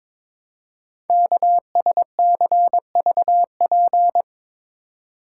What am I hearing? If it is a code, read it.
Morse code, heard as a keyed tone, about 22 wpm, 700 Hz. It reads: KSCVP
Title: Code Groups - 5 Letters - 22wpm